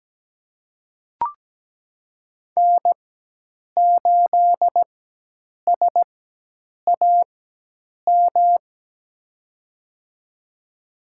Morse code, heard as a keyed tone, about 17 wpm, spaced as for 8 wpm, 700 Hz. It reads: N8SAM
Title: USA Call Signs - 4x Character Spacing - 17wpm